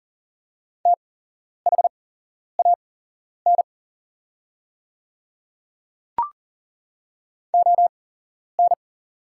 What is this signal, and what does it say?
Morse code, heard as a keyed tone, about 39 wpm, spaced as for 11 wpm, 690 Hz. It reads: THAN ON
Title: 8x Character Spacing - Top 200 Words - 40wpm